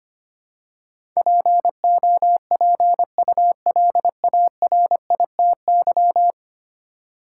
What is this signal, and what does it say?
Morse code, heard as a keyed tone, about 25 wpm, 700 Hz. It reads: POPULARITY
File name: 10 Letter Words - First 3 Letters Given - 25wpm